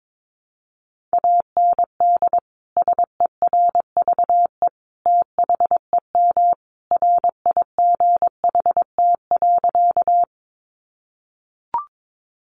Morse code, heard as a keyed tone, about 22 wpm, 700 Hz. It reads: AND SERVE THEM RIGHT.
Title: Sentences from Top 400 Words 22wpm